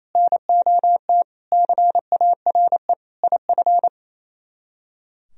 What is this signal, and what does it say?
Morse code, heard as a keyed tone, about 32 wpm, 700 Hz. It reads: NOT CARE IF
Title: Speed-Racing - Sets of 5 Words 28wpm